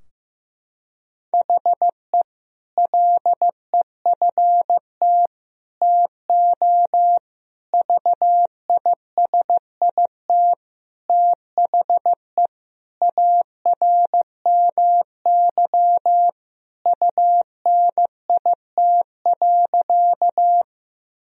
Morse code, heard as a keyed tone, about 15 wpm, 700 Hz. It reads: HE LEFT TO VISIT THE ARMY UNIT.
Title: Sentences from Top 1000 Words - Review Part 2 - 15wpm